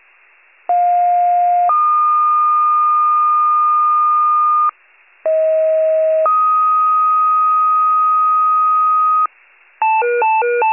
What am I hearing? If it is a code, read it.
Two-tone page, heard as a one-second tone, then a three-second tone, about 710 then 1220 Hz, about 640 then 1250 Hz. It is heard from 0.7 to 9.3 s.